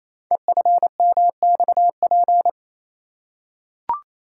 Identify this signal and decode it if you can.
Morse code, heard as a keyed tone, about 27 wpm, 700 Hz. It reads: EFMXP
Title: Code Groups - 5 Letters - 28wpm